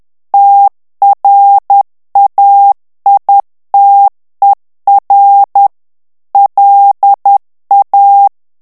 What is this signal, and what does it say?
Morse code, heard as a keyed tone, about 10 wpm, 800 Hz. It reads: TRAITER LA
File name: morse mp3 / FRANCAIS TEXTE CLAIR 2 VITESSE 10 MOTS MINUTES